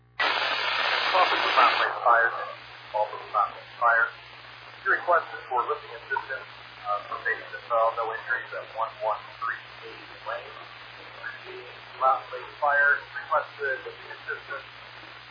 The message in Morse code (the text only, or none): none